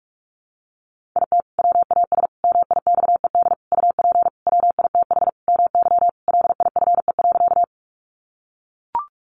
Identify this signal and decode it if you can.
Morse code, heard as a keyed tone, about 44 wpm, 700 Hz. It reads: IT WAS MIXED UP WITH MY LIFE.
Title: Sentences from Top 1500 Words 45wpm